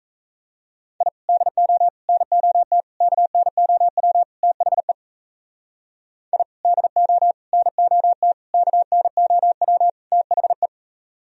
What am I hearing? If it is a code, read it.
Morse code, heard as a keyed tone, about 37 wpm, 700 Hz. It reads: I DO NOT KNOW THE I DO NOT KNOW THE